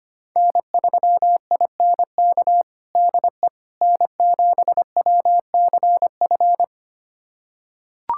Morse code, heard as a keyed tone, about 25 wpm, 700 Hz. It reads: N3INK DE N7WCF
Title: USA Callsign DE Callsign - 5 Characters - 25wpm